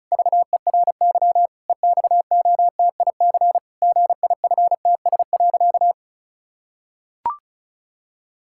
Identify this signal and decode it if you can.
Morse code, heard as a keyed tone, about 36 wpm, 700 Hz. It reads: VERY EXOTIC GIFTS.
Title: Pangram Challenge v2 - 35wpm